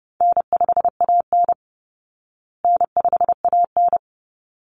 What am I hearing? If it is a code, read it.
Morse code, heard as a keyed tone, about 30 wpm, 700 Hz. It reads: N5AN N5AN
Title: Call Signs Repeated 3x - 30wpm